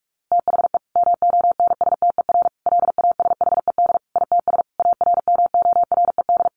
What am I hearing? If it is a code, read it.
Morse code, heard as a keyed tone, about 44 wpm, 700 Hz. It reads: THE MONSTER LASHED ITS ARMORED
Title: A Sound Of Thunder 45wpm